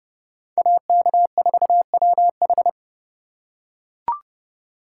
Morse code, heard as a keyed tone, about 29 wpm, 700 Hz. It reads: AK4WH